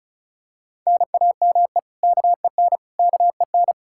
Morse code, heard as a keyed tone, about 34 wpm, 700 Hz. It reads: NAME KEN KEN
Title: Practice QSO - Exchange 1 - Beginning - 35wpm